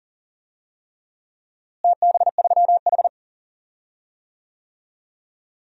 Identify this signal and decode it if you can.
Morse code, heard as a keyed tone, about 39 wpm, 700 Hz. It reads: TB3H
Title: DXCC-100 Call Signs - 4 Characters - Repeated 3x - 40wpm